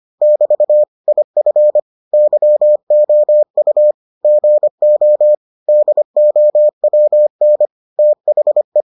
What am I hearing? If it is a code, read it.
Morse code, heard as a keyed tone, about 25 wpm, 600 Hz. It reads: <BT> IF YOU GO DOWN THE